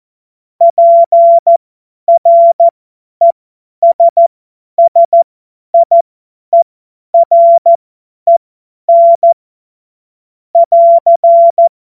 Morse code, heard as a keyed tone, about 14 wpm, spaced as for 10 wpm, 680 Hz. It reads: PRESSIEREN <AR>
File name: Morsekurs / Witze / Witz 31.Oktober 2023 50 F70